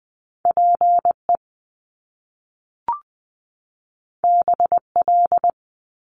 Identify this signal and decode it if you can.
Morse code, heard as a keyed tone, about 20 wpm, 700 Hz. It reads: PE BL
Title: Top Two Letter Combinations in Words 20wpm